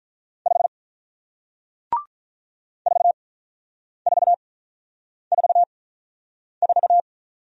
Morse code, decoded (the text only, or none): 5 4444